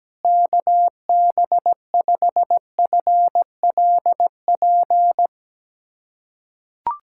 Morse code, heard as a keyed tone, about 17 wpm, 700 Hz. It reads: KB5FLP
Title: USA Callsign DE Callsign - 6 Characters - 17wpm